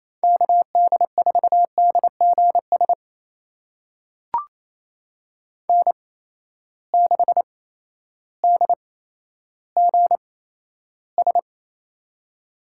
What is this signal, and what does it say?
Morse code, heard as a keyed tone, about 28 wpm, spaced as for 13 wpm, 700 Hz. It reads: KD4DGS N6DGS
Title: USA Call Signs - 8x Character Spacing - 28wpm